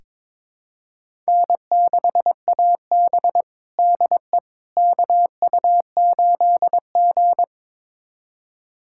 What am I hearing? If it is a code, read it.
Morse code, heard as a keyed tone, about 22 wpm, 700 Hz. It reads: N6AB DE KU8G
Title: USA Callsign DE Callsign - 4 Characters - 22wpm